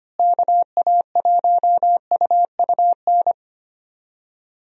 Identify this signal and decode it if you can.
Morse code, heard as a keyed tone, about 25 wpm, 700 Hz. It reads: KA1UUN
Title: USA Call Signs - 6 Characters - Repeated 2x - 25wpm